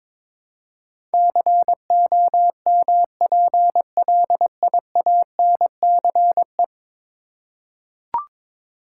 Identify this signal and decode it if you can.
Morse code, heard as a keyed tone, about 22 wpm, 700 Hz. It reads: COMPLIANCE
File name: practice / 22wpm / 10 Letter Words - First 3 Letters Given - 22wpm